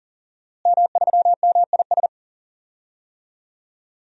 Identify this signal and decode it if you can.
Morse code, heard as a keyed tone, about 40 wpm, 700 Hz. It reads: M3MIS
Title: DXCC-100 Call Signs - 5 Characters - Repeated 2x - 40wpm